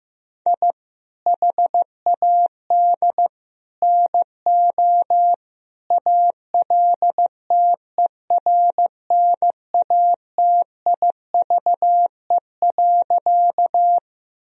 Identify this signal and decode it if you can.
Morse code, heard as a keyed tone, about 15 wpm, 700 Hz. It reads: I HAD NO ALTERNATIVE.
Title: Sentences from Top 1000 Words - Review Part 4 - 15wpm